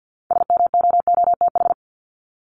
Morse code, heard as a keyed tone, about 50 wpm, 700 Hz. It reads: SMOOTH
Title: Top 1251-1500 Words 50wpm